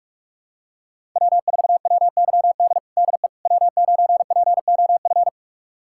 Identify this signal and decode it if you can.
Morse code, heard as a keyed tone, about 46 wpm, 700 Hz. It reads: W4WYD DE W9POF
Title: USA Callsign DE Callsign - 5 Characters - 45wpm